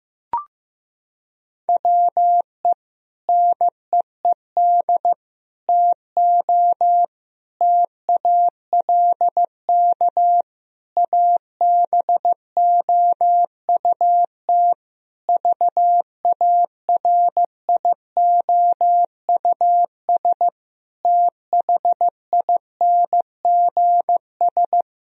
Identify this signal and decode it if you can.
Morse code, heard as a keyed tone, about 15 wpm, 700 Hz. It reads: WE NEED TO TALK ABOUT VARIOUS THINGS
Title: Sentences from Top 1000 Words - Review Part 4 - 15wpm